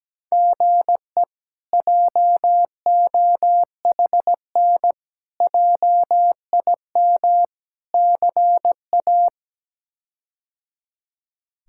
Morse code, heard as a keyed tone, about 17 wpm, 700 Hz. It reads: GE JOHN JIM CA
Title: K1USN SST QSO Practice - 17wpm